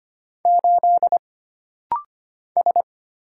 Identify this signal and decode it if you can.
Morse code, heard as a keyed tone, about 25 wpm, 700 Hz. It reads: 8 S
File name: Single Character (Letter:Number:Punctuation:Prosign) - Rapid-Fire - 25wpm